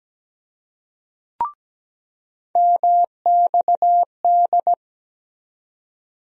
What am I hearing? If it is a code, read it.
Morse code, heard as a keyed tone, about 17 wpm, 700 Hz. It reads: MXD
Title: USA Call Sign Postfixes - 3 Letters 17wpm